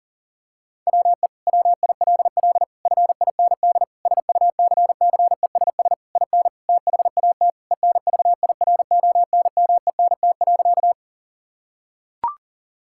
Morse code, heard as a keyed tone, about 40 wpm, 700 Hz. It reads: WE WILL FIND SUCCESS IN THAT ENVIRONMENT.
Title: Sentences from Top 1000 Words - Review Part 5 - 40wpm